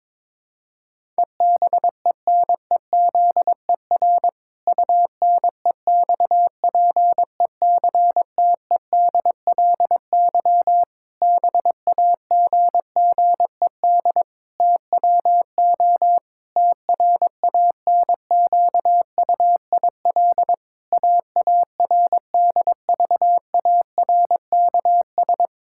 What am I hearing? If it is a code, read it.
Morse code, heard as a keyed tone, about 22 wpm, 700 Hz. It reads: EBENEZER UNEXPECTEDLY BAGGED TWO TRANQUIL AARDVARKS
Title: Pangram Challenge v2 - 22wpm